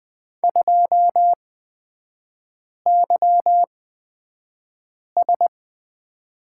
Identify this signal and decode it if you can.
Morse code, heard as a keyed tone, about 20 wpm, spaced as for 5 wpm, 700 Hz. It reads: 2YS
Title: 224 Lesson 36e - BT - Three Characters 20wpm